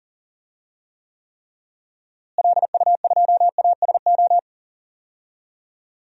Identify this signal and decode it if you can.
Morse code, heard as a keyed tone, about 40 wpm, 700 Hz. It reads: LU2ASO
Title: DXCC-100 Call Signs - 6 Characters - Repeated 3x - 40wpm